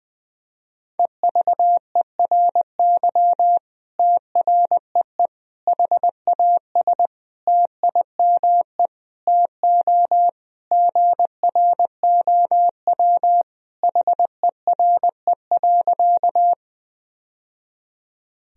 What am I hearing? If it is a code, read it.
Morse code, heard as a keyed tone, about 21 wpm, 700 Hz. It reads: EVERY TREE HAS TIME TO GROW HERE.
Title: Sentences from Top 1000 Words - Review Part 1 - 20wpm